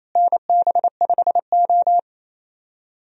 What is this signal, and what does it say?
Morse code, heard as a keyed tone, about 28 wpm, 700 Hz. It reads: NB5O
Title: USA Call Signs - 4 Characters - Repeated 2x - 28wpm